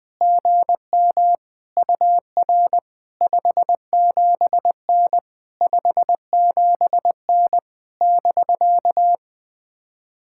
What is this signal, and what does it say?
Morse code, heard as a keyed tone, about 20 wpm, 700 Hz. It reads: GM UR 57N 57N <BK>
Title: POTA QSO Practice - 20wpm